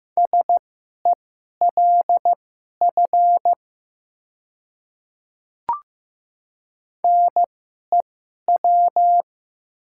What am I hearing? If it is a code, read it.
Morse code, heard as a keyed tone, about 15 wpm, spaced as for 11 wpm, 700 Hz. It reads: SELF NEW
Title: 2x Character Spacing - Top 300 Words - 15wpm